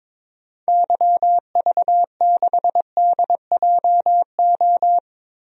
Morse code, heard as a keyed tone, about 22 wpm, 700 Hz. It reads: YV6DJO
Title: DXCC-100 Call Signs - 6 Characters - 22wpm